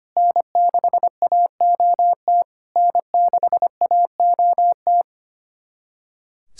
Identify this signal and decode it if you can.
Morse code, heard as a keyed tone, about 25 wpm, 700 Hz. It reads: N6AOT N6AOT
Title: SOTA QSO Practice 25wpm